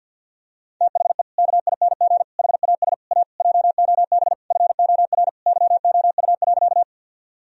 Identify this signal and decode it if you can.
Morse code, heard as a keyed tone, about 52 wpm, 700 Hz. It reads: THE KING HAS A JOB FOR YOU.
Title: Sentences from Top 1000 Words - Review Part 3 - 50wpm